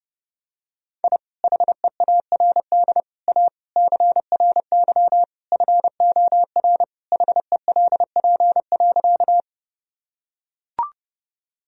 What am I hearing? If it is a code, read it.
Morse code, heard as a keyed tone, about 30 wpm, 700 Hz. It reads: I HEARD A CRY FOR HELP.
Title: Sentences from Top 1000 Words - Review Part 2 - 30wpm